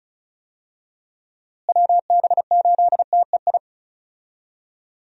Morse code, heard as a keyed tone, about 36 wpm, 700 Hz. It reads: WB8TEI
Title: USA Call Signs - 6 Characters - Repeated 2x - 35wpm